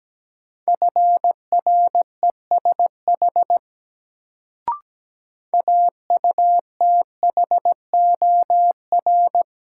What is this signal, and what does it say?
Morse code, heard as a keyed tone, about 17 wpm, 700 Hz. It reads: FRESH AUTHOR
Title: Sentences from Top 1200 Words - New Word Drill - 17wpm